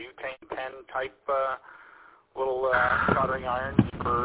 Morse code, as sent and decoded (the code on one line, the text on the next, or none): none